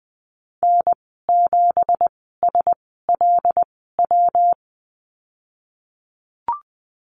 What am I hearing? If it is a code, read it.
Morse code, heard as a keyed tone, about 20 wpm, spaced as for 15 wpm, 700 Hz. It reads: N7SLW
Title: 167 Lesson 27g - Number 7 - Call Signs 20wpm